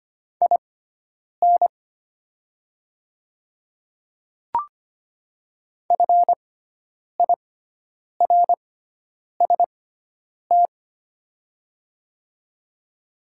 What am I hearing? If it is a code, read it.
Morse code, heard as a keyed tone, about 25 wpm, spaced as for 9 wpm, 700 Hz. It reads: IN FIRST
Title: 6x Character Spacing - Top 300 Words - 25wpm